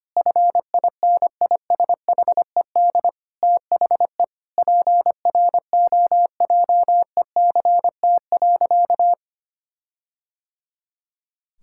Morse code, heard as a keyed tone, about 25 wpm, 700 Hz. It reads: FINISHED THE PROJECT.
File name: practice / 25wpm / Sentences with 4-Word Phrases - 25wpm